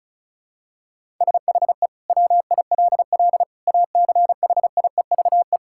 Morse code, heard as a keyed tone, about 36 wpm, 700 Hz. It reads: SHE WILL ACHIEVE
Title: Sentences from Top 1000 Words - Review Part 6 - 35wpm